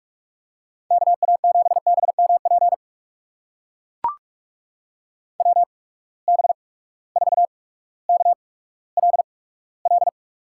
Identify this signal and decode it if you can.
Morse code, heard as a keyed tone, about 44 wpm, spaced as for 19 wpm, 700 Hz. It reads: KA7BMP WB4KLL
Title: USA Call Signs - 6 Characters - 8x Character Spacing - 45wpm